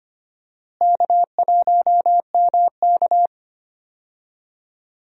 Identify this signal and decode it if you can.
Morse code, heard as a keyed tone, about 25 wpm, 700 Hz. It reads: K1MK
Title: USA Call Signs - 4 Characters - Repeated 3x - 25wpm